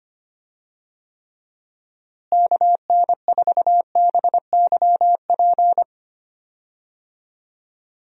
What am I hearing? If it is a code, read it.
Morse code, heard as a keyed tone, about 25 wpm, 700 Hz. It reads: KN4BYP
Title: USA Call Signs - 6 Characters - Repeated 2x - 25wpm